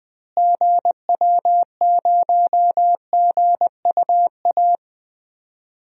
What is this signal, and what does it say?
Morse code, heard as a keyed tone, about 21 wpm, 700 Hz. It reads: GW0GUA
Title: DXCC-100 Call Signs - 6 Characters - Repeated 2x - 20wpm